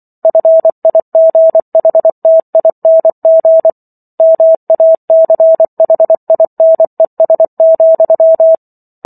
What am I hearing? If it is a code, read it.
Morse code, heard as a keyed tone, about 24 wpm, 650 Hz. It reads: FIGHTING MACHINES,